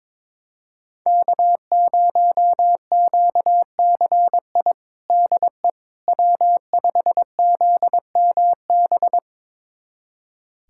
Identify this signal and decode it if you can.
Morse code, heard as a keyed tone, about 22 wpm, 700 Hz. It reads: K0QCI DE W5ZMB
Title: USA Callsign DE Callsign - 5 Characters - 22wpm